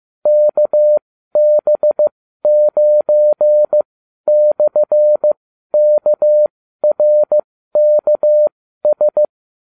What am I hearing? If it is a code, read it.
Morse code, heard as a keyed tone, about 15 wpm, spaced as for 12 wpm, 600 Hz. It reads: KB9/KRKS